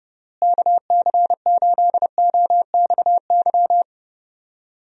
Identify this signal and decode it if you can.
Morse code, heard as a keyed tone, about 30 wpm, 700 Hz. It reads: KC8OXY
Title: USA Call Signs - 6 Characters - Repeated 3x - 30wpm